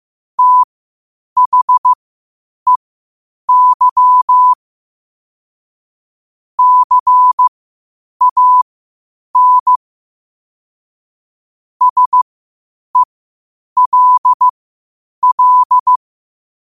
Morse code, heard as a keyed tone, about 15 wpm, spaced as for 8 wpm, 1000 Hz. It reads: THEY CAN SELL
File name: qotdfiles / 8wpm